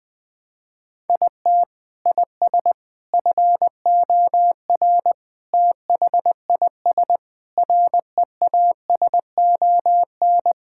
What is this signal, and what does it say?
Morse code, heard as a keyed tone, about 20 wpm, 700 Hz. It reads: IT IS FOR THIS REASON
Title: Sets of 6 Words - 20wpm